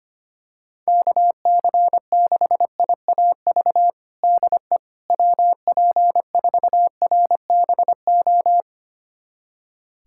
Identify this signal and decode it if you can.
Morse code, heard as a keyed tone, about 25 wpm, 700 Hz. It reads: KC6IAV DE WP4RBO
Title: USA Callsign DE Callsign 25wpm